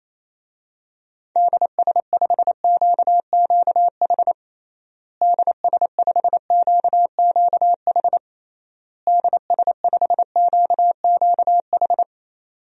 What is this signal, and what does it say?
Morse code, heard as a keyed tone, about 28 wpm, 700 Hz. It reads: DS5QQH DS5QQH DS5QQH
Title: DXCC-100 Call Signs - 6 Characters - Repeated 3x - 28wpm